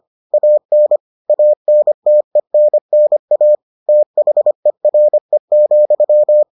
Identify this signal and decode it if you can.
Morse code, heard as a keyed tone, about 25 wpm, 600 Hz. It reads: AN ANTENNA THERE,